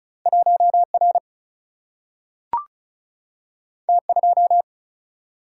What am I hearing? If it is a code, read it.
Morse code, heard as a keyed tone, about 36 wpm, 700 Hz. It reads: 1R T2
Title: Single-Letter Single-Number 35wpm